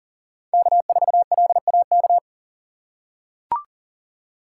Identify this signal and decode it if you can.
Morse code, heard as a keyed tone, about 40 wpm, 690 Hz. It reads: K4LAK
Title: USA Call Signs - 6x Character Spacing - 40wpm